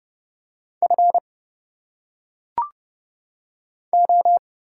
Morse code, heard as a keyed tone, about 30 wpm, 700 Hz. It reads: F O
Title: Single Letters - Phonetic Pronunciation 30wpm